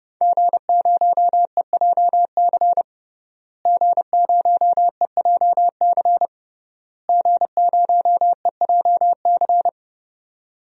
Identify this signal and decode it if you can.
Morse code, heard as a keyed tone, about 32 wpm, 700 Hz. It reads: G0EJC G0EJC G0EJC
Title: DXCC-100 Call Signs - 5 Characters - Repeated 3x - 30wpm